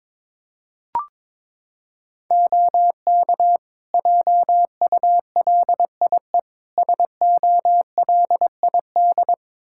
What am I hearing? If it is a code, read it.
Morse code, heard as a keyed tone, about 22 wpm, 700 Hz. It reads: OK JULIE SOLID
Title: Practice QSO - Exchange 3 - More Info - 22wpm